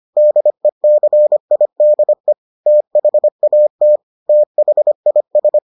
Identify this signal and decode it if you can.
Morse code, heard as a keyed tone, about 25 wpm, 600 Hz. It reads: DECIDE THAT THIS